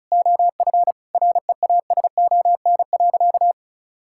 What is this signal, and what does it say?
Morse code, heard as a keyed tone, about 36 wpm, 700 Hz. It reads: OF REASON.
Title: Sentences from Top 1000 Words - Review Part 6 - 35wpm